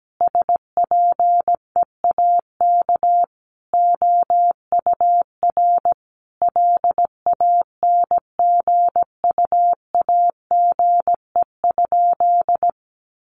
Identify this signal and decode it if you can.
Morse code, heard as a keyed tone, about 17 wpm, 700 Hz. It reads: SPEAK OUR LANGUAGE?
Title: Sentences from Top 1100 Words 17wpm